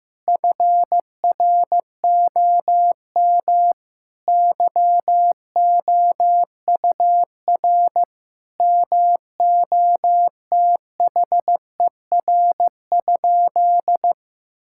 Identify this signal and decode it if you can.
Morse code, heard as a keyed tone, about 16 wpm, 700 Hz. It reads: FROM YOUR MOTHER?
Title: Sentences from Top 1600 Words 15wpm